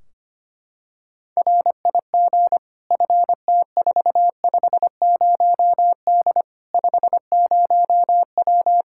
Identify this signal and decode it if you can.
Morse code, heard as a keyed tone, about 25 wpm, 700 Hz. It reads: RIG FT450D 50W
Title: Practice QSO - Exchange 2 - Info - 25wpm